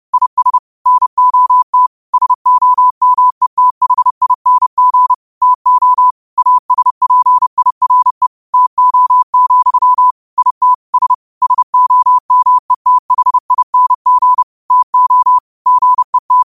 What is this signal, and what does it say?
Morse code, heard as a keyed tone, about 30 wpm, 1000 Hz. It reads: IS NOT SOMETHING TO ASPIRE TO, IT S SOMETHING TO GET
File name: qotdfiles / 30wpm